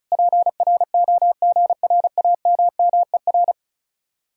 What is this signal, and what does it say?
Morse code, heard as a keyed tone, about 43 wpm, 700 Hz. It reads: PROGRAMMER
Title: Speed-Racing Common QSO Phrases - 35wpm